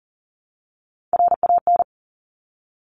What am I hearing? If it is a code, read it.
Morse code, heard as a keyed tone, about 40 wpm, 700 Hz. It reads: RAN
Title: Top 500 Words Repeated 40wpm